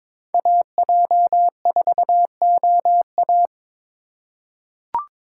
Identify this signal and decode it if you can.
Morse code, heard as a keyed tone, about 22 wpm, 700 Hz. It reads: AJ4OA
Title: USA Call Signs - 3x Character Spacing - 22wpm